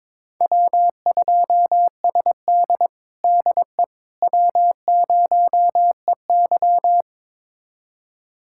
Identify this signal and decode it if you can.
Morse code, heard as a keyed tone, about 22 wpm, 700 Hz. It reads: W2SD DE W0EY
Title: USA Callsign DE Callsign - 4 Characters - 22wpm